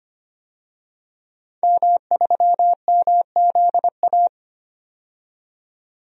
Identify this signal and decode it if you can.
Morse code, heard as a keyed tone, about 25 wpm, 700 Hz. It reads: M3MZA
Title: DXCC-100 Call Signs - 5 Characters - 25wpm